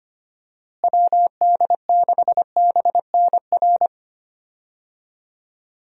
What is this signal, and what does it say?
Morse code, heard as a keyed tone, about 25 wpm, 700 Hz. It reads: WD6BNR
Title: USA Call Signs - 6 Characters - Repeated 3x - 25wpm